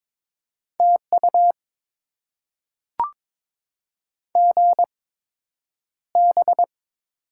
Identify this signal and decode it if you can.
Morse code, heard as a keyed tone, about 22 wpm, spaced as for 9 wpm, 700 Hz. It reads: TU GB
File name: Top 90 QSO Elements - 8x Character Spacing - 22wpm